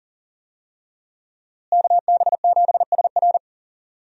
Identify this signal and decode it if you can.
Morse code, heard as a keyed tone, about 37 wpm, 700 Hz. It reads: KB7SR
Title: USA Call Signs - 5 Characters - Repeated 3x - 40wpm